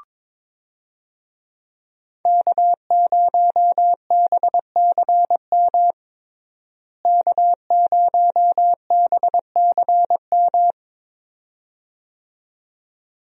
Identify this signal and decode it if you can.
Morse code, heard as a keyed tone, about 21 wpm, 700 Hz. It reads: K0BCM K0BCM
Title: USA Call Signs - 5 Characters - Repeated 2x - 22wpm